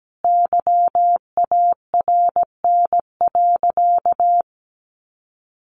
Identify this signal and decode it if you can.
Morse code, heard as a keyed tone, about 17 wpm, 700 Hz. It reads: YARN.
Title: Easy Phrases 17wpm